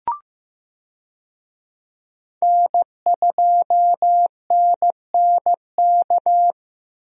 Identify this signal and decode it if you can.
Morse code, heard as a keyed tone, about 15 wpm, 700 Hz. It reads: N2NNK
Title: USA Call Signs - 5 Characters - Repeated 2x - 15wpm